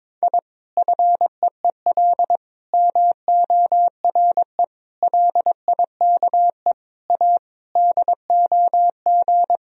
Morse code, heard as a keyed tone, about 22 wpm, 700 Hz. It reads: I FEEL MORE LIKE A DOG
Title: Sentences from Top 1000 Words - Review Part 2 - 22wpm